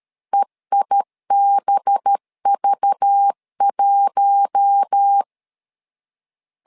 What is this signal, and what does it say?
Morse code, heard as a keyed tone, about 13 wpm, 800 Hz. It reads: EIBV1
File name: LEZIONI / 5 LBVC / 5 LBV 12wpm 1855